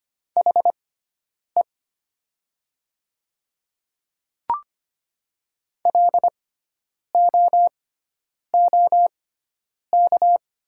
Morse code, heard as a keyed tone, about 25 wpm, spaced as for 9 wpm, 700 Hz. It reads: HE LOOK
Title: 6x Character Spacing - Top 100 Words - 25wpm